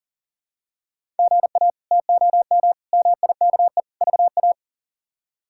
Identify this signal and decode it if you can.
Morse code, heard as a keyed tone, about 40 wpm, 700 Hz. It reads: GA TOM MIKE VA
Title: K1USN SST QSO Practice - 40wpm